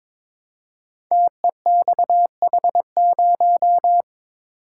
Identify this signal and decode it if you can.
Morse code, heard as a keyed tone, about 22 wpm, 700 Hz. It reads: TEXH0
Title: Code Groups - 5 Letter:Numbers - 22wpm